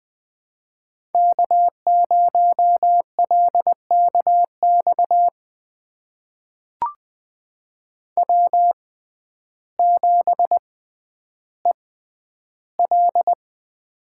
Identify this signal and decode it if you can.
Morse code, heard as a keyed tone, about 20 wpm, spaced as for 11 wpm, 700 Hz. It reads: K0LKX W7EL
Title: USA Call Signs - 5 Characters - 6x Character Spacing - 20wpm